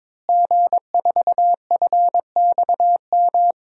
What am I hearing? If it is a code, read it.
Morse code, heard as a keyed tone, about 22 wpm, 700 Hz. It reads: G4FXM
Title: DXCC-100 Call Signs - 5 Characters - Repeated 2x - 22wpm